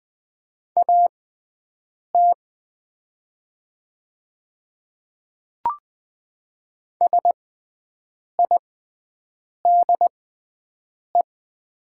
Morse code, heard as a keyed tone, about 20 wpm, spaced as for 7 wpm, 690 Hz. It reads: AT SIDE
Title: 6x Character Spacing - Top 100 Words - 20wpm